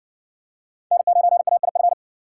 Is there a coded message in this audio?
Yes: Morse code from 0.9 to 1.9 s, heard as a keyed tone, about 52 wpm, 700 Hz. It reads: N9AIF